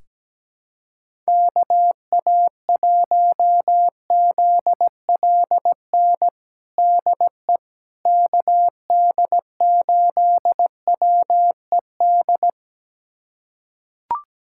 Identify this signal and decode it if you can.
Morse code, heard as a keyed tone, about 17 wpm, 700 Hz. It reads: KA1ZLN DE KD8WED
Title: USA Callsign DE Callsign - 6 Characters - 17wpm